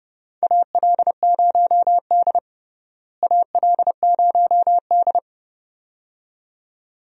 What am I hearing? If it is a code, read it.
Morse code, heard as a keyed tone, about 30 wpm, 700 Hz. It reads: AL0D AL0D
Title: USA Call Signs - 4 Characters - Repeated 2x - 30wpm